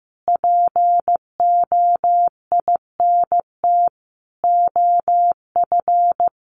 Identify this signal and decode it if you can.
Morse code, heard as a keyed tone, about 15 wpm, 700 Hz. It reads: POINT OF
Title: Sentences from Top 1500 Words 15wpm